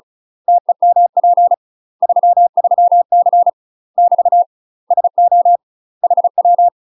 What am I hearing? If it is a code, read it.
Morse code, heard as a keyed tone, about 36 wpm, 700 Hz. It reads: TEMP 33C <BT> SO HW
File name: UK MASTER QSOs TRAINING / 35WPM / UK MASTER QSOs 0002